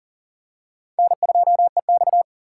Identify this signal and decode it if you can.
Morse code, heard as a keyed tone, about 40 wpm, 700 Hz. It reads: N2EX
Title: USA Call Signs - 4 Characters - 6x Character Spacing - 40wpm